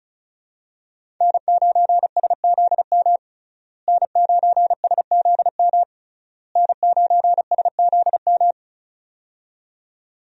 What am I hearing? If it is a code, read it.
Morse code, heard as a keyed tone, about 36 wpm, 700 Hz. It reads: N9SZM N9SZM N9SZM
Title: USA Call Signs - 5 Characters - Repeated 3x - 35wpm